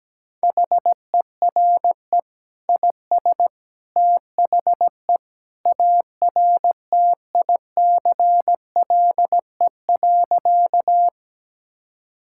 Morse code, heard as a keyed tone, about 17 wpm, 700 Hz. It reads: HERE IS THE ARTICLE.
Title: Sentences from Top 1000 Words - Review Part 5 - 17wpm